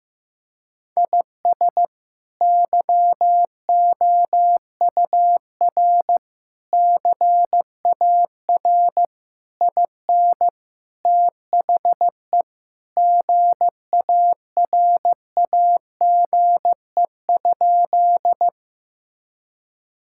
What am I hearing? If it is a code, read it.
Morse code, heard as a keyed tone, about 15 wpm, 700 Hz. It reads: IS YOUR CAR IN THE GARAGE?
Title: Sentences from Top 1600 Words 15wpm